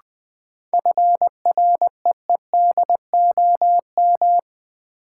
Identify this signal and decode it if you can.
Morse code, heard as a keyed tone, about 20 wpm, 700 Hz. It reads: FREEDOM
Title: Sentences from Top 800 Words - New Word Drill - 20wpm